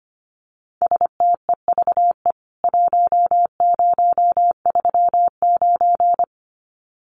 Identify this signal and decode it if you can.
Morse code, heard as a keyed tone, about 25 wpm, 700 Hz. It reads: STEVE 1039
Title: CWT Exchange 25wpm